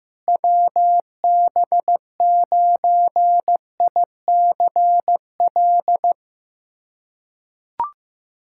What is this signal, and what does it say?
Morse code, heard as a keyed tone, about 15 wpm, 700 Hz. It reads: WB9ICL
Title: SOTA QSO Practice 15wpm